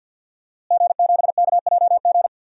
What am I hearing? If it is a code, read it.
Morse code, heard as a keyed tone, about 52 wpm, 700 Hz. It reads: G7KJG
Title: DXCC-100 Call Signs - 5 Characters - Repeated 3x - 50wpm